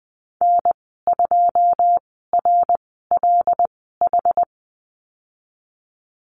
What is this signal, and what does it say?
Morse code, heard as a keyed tone, about 20 wpm, spaced as for 15 wpm, 700 Hz. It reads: N2RLH
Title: 088 Lesson 15g - Number 5 - Call Signs 20wpm